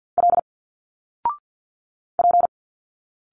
Morse code, heard as a keyed tone, about 50 wpm, 700 Hz. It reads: L P
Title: Single Letters 50wpm